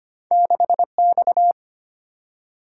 Morse code, heard as a keyed tone, about 25 wpm, 690 Hz. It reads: 6X
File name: Two Letter-Number Combinations - 25wpm